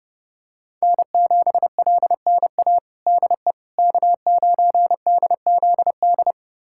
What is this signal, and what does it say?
Morse code, heard as a keyed tone, about 30 wpm, 700 Hz. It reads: N7LNA DE K9DZD
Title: USA Callsign DE Callsign - 5 Characters - 30wpm